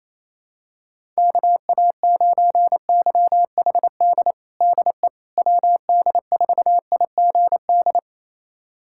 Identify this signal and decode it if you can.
Morse code, heard as a keyed tone, about 28 wpm, 700 Hz. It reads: KA9YHD DE WD4IGD
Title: USA Callsign DE Callsign 28wpm